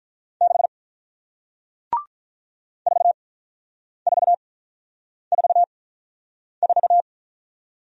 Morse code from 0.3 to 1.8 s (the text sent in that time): 6